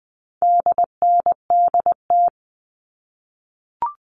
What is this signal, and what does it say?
Morse code, heard as a keyed tone, about 19 wpm, 700 Hz. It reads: DNDT
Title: All Cut Numbers - 4-Digit 20wpm